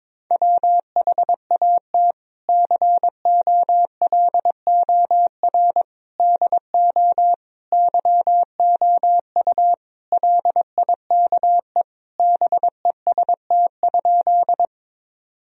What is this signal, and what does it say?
Morse code, heard as a keyed tone, about 22 wpm, 700 Hz. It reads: WHAT COLOR DO YOU LIKE BEST?
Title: Sentences from Top 1000 Words - Review Part 2 - 22wpm